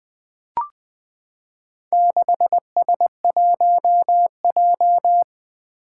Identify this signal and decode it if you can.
Morse code, heard as a keyed tone, about 20 wpm, 700 Hz. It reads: 6S1J
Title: Four Challenging Letter-Number Combinations 20wpm